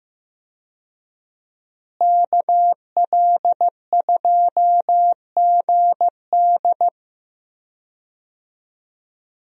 Morse code, heard as a keyed tone, about 15 wpm, 700 Hz. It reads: KL2GD
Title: USA Call Signs - 5 Characters - Repeated 2x - 15wpm